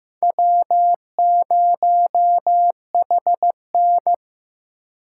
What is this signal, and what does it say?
Morse code, heard as a keyed tone, about 18 wpm, 700 Hz. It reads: W0HN
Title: Speed Racing - USA 1x2 Call Signs 15wpm